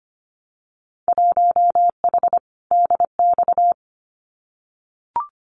Morse code, heard as a keyed tone, about 25 wpm, 700 Hz. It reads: 1H DX
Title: Winter Field Day Exchanges 25wpm